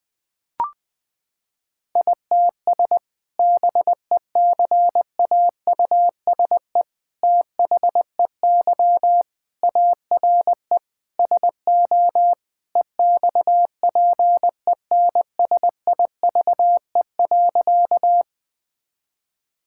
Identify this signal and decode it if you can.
Morse code, heard as a keyed tone, about 19 wpm, 700 Hz. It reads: ITS BECAUSE THEY ARE SO EXPENSIVE.
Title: Sentences from Top 1000 Words - Review Part 6 - 20wpm